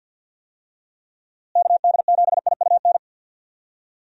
Morse code, heard as a keyed tone, about 48 wpm, 700 Hz. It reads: KD7IUN
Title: USA Call Signs - 6 Characters - Repeated 3x - 50wpm